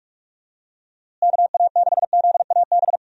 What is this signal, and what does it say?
Morse code, heard as a keyed tone, about 46 wpm, 700 Hz. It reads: KA6ZAB
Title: USA Call Signs - 6 Characters - Repeated 2x - 45wpm